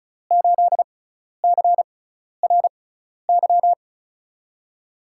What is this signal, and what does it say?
Morse code, heard as a keyed tone, about 34 wpm, spaced as for 12 wpm, 700 Hz. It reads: 8CRY
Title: USA Call Signs - 6x Character Spacing - 35wpm